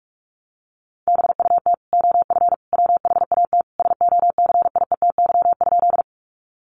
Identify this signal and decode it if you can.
Morse code, heard as a keyed tone, about 44 wpm, 700 Hz. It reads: BUT OF WHAT SOCIETY?
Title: Sentences from Top 700 Words 45wpm